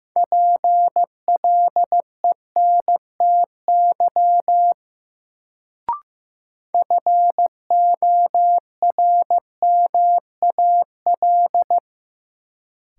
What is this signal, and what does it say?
Morse code, heard as a keyed tone, about 15 wpm, 700 Hz. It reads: PLENTY FORMAL
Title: Sentences from Top 1300 Words - New Word Drill - 15wpm